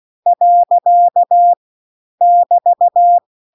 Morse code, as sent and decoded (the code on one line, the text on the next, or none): .-.-.- / -...-
. <BT>